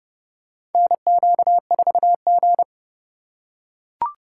Morse code, heard as a keyed tone, about 30 wpm, 700 Hz. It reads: NQ4G
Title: USA Call Signs - 4 Characters - 2x Character Spacing - 30wpm